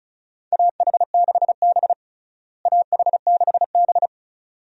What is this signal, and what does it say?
Morse code, heard as a keyed tone, about 36 wpm, 700 Hz. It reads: AH6B AH6B
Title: USA Call Signs - 4 Characters - Repeated 2x - 35wpm